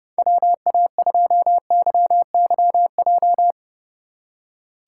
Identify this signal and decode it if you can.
Morse code, heard as a keyed tone, about 30 wpm, 700 Hz. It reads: WA2YYJ
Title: USA Call Signs - 3x Character Spacing - 30wpm